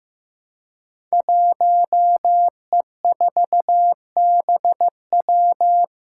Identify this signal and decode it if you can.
Morse code, heard as a keyed tone, about 15 wpm, 700 Hz. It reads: 1E4BW
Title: Five Challenging Letter-Number Combinations 15wpm